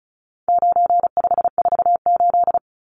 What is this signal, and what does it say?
Morse code, heard as a keyed tone, about 34 wpm, 700 Hz. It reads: 9548
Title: Numbers - 4-Digit 35wpm